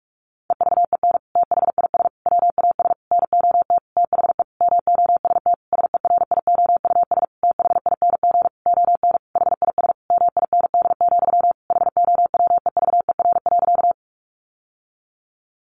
Morse code, heard as a keyed tone, about 44 wpm, 700 Hz. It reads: EVEN THIS WAS NOT THE MOST SERIOUS THING ON HIS MIND, HOWEVER.
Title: Sentences from Top 1100 Words 45wpm